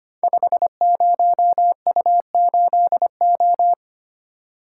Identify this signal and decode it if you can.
Morse code, heard as a keyed tone, about 25 wpm, 700 Hz. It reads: 50U8O
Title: Code Groups - 5 Letter:Numbers - 25wpm